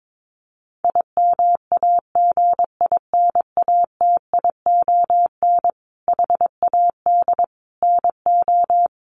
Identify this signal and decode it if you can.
Morse code, heard as a keyed tone, about 22 wpm, 700 Hz. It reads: IMAGINATION HAD NO
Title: Sentences from Top 1300 Words 22wpm